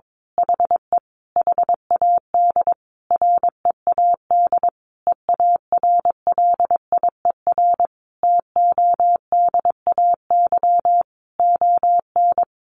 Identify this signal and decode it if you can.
Morse code, heard as a keyed tone, about 22 wpm, 700 Hz. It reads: HE HAD READ EARLIER TODAY ON
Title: A Sound Of Thunder 22wpm